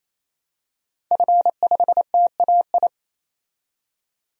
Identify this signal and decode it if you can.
Morse code, heard as a keyed tone, about 28 wpm, 700 Hz. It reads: F5TAI
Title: DXCC-100 Call Signs - 5 Characters - 28wpm